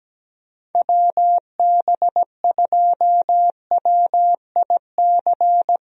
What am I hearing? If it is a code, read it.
Morse code, heard as a keyed tone, about 17 wpm, 700 Hz. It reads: WB2WIC
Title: USA Call Signs - 6 Characters - Repeated 3x - 17wpm